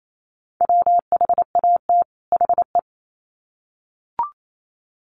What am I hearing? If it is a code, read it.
Morse code, heard as a keyed tone, about 28 wpm, 700 Hz. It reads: WHAT HE
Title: Sets of 2 Words 28wpm